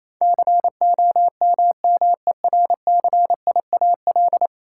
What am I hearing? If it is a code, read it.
Morse code, heard as a keyed tone, about 28 wpm, 700 Hz. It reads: COMMERCIAL
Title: Sentences from Top 1000 Words - Review Part 5 - New Word Drill - 28wpm